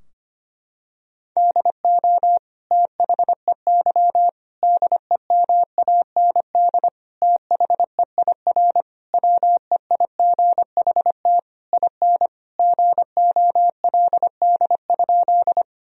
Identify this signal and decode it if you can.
Morse code, heard as a keyed tone, about 25 wpm, 700 Hz. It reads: DO THEY DEMAND THEIR WEIGHT IN GOLD?
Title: Sentences from Top 1000 Words - Review Part 3 - 25wpm